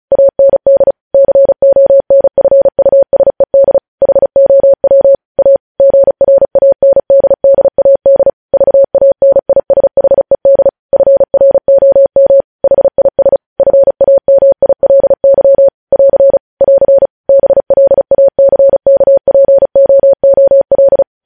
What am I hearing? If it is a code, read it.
Morse code, heard as a keyed tone, about 36 wpm, 550 Hz. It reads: AND CONFUSED HOW A GRANDDAD VANISHED FROM HIS FAMILY <AR> <AR> BLACKPOOL
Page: News headlines in Morse code at 35 wpm